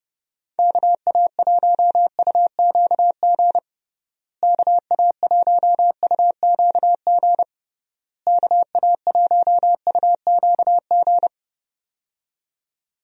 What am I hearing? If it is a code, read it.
Morse code, heard as a keyed tone, about 32 wpm, 700 Hz. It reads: KA1UQG KA1UQG KA1UQG
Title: USA Call Signs - 6 Characters - Repeated 3x - 30wpm